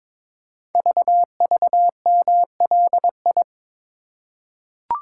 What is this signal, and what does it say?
Morse code, heard as a keyed tone, about 22 wpm, 700 Hz. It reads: VVMLI
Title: Code Groups - 5 Letters - 22wpm